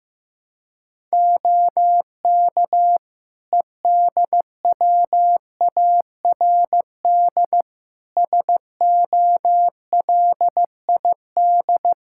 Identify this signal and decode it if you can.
Morse code, heard as a keyed tone, about 15 wpm, 700 Hz. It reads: OK EDWARD SOLID
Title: Practice QSO - Exchange 3 - More Info - 15wpm